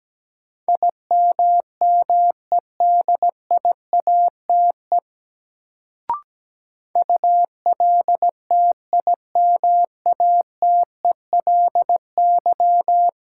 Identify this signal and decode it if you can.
Morse code, heard as a keyed tone, about 17 wpm, 700 Hz. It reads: IMMEDIATE ULTIMATELY
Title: Sentences from Top 1200 Words - New Word Drill - 17wpm